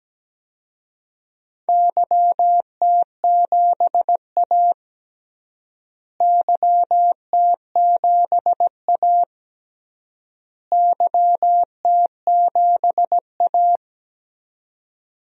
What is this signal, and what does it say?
Morse code, heard as a keyed tone, about 17 wpm, 700 Hz. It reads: YT7A YT7A YT7A
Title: DXCC-100 Call Signs - 4 Characters - Repeated 3x - 17wpm